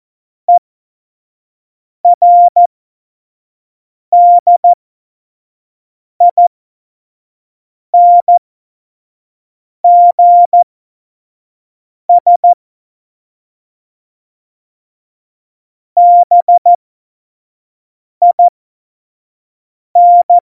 Morse code, heard as a keyed tone, about 14 wpm, spaced as for 5 wpm, 700 Hz. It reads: ERDINGS BIN
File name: Morsekurs / Witze / Witz 26 Mai 2020, 25 F70